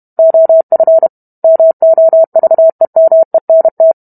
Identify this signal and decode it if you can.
Morse code, heard as a keyed tone, about 32 wpm, 650 Hz. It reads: OF MOVEMENT